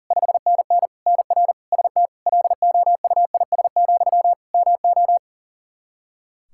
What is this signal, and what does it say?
Morse code, heard as a keyed tone, about 40 wpm, 700 Hz. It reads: 5NN NR ST LOUIS, MO
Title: Practice QSO - Exchange 1 - Beginning - 40wpm